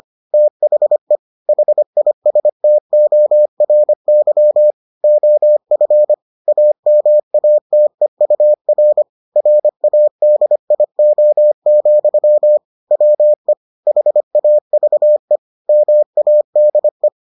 Morse code, heard as a keyed tone, about 25 wpm, 600 Hz. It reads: THE HISTORY OF AMATEUR RADIO, WE HAVE MADE